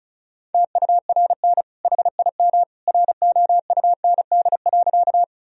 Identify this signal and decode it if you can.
Morse code, heard as a keyed tone, about 34 wpm, 700 Hz. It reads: TURN HIM ROUND.
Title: Sentences from Top 1000 Words - Review Part 1 - 35wpm